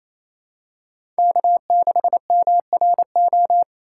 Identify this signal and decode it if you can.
Morse code, heard as a keyed tone, about 28 wpm, 700 Hz. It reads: K6MRO
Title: USA Call Signs - 5 Characters - 4x Character Spacing - 28wpm